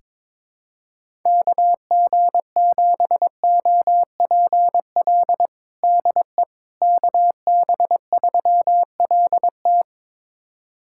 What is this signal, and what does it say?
Morse code, heard as a keyed tone, about 22 wpm, 700 Hz. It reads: KG7OPL DE KB3LT
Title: USA Callsign DE Callsign 22wpm